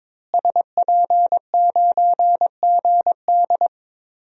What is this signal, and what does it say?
Morse code, heard as a keyed tone, about 22 wpm, 700 Hz. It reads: SP9GD
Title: DXCC-100 Call Signs - 5 Characters - Repeated 3x - 22wpm